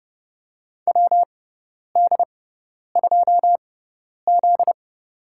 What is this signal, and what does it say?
Morse code, heard as a keyed tone, about 31 wpm, spaced as for 10 wpm, 700 Hz. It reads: WD2Z
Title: USA Call Signs - 4 Characters - 6x Character Spacing - 30wpm